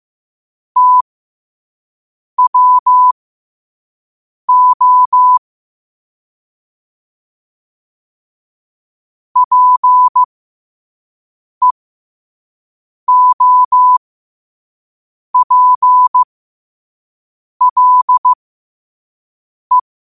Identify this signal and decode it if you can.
Morse code, heard as a keyed tone, about 15 wpm, spaced as for 5 wpm, 1000 Hz. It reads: TWO PEOPLE
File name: qotdfiles / 5wpm